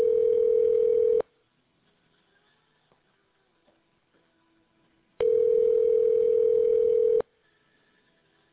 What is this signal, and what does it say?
Ringback tone (North America), 440 + 480 Hz.